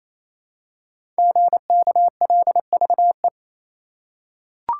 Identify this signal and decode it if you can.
Morse code, heard as a keyed tone, about 27 wpm, 700 Hz. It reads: GKLVE
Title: Code Groups - 5 Letter:Numbers - 28wpm